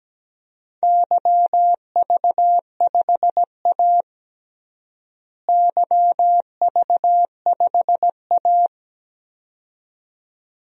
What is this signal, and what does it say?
Morse code, heard as a keyed tone, about 17 wpm, 700 Hz. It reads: YV5A YV5A
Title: DXCC-100 Call Signs - 4 Characters - Repeated 2x - 17wpm